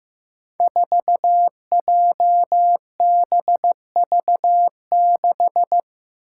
Morse code, heard as a keyed tone, about 15 wpm, 700 Hz. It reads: 4JBV6
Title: Five Challenging Letter-Number Combinations 15wpm